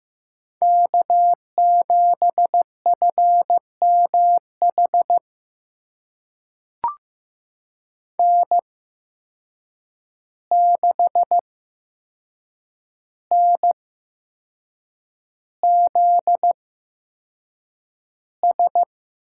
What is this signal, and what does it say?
Morse code, heard as a keyed tone, about 15 wpm, spaced as for 6 wpm, 700 Hz. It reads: K7FMH N6NZS
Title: USA Call Signs - 5 Characters - 8x Character Spacing - 15wpm